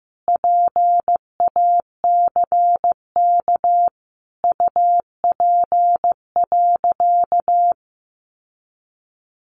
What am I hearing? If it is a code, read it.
Morse code, heard as a keyed tone, about 15 wpm, 700 Hz. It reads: PACK UP.
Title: Sentences from Top 1300 Words 15wpm